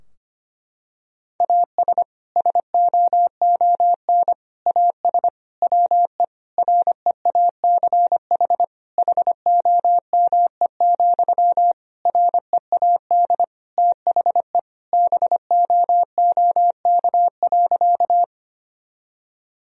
Morse code, heard as a keyed tone, about 25 wpm, 700 Hz. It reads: AS SOON AS WE REACH HOME, READ THE BOOK.
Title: Sentences from Top 1000 Words - Review Part 2 - 25wpm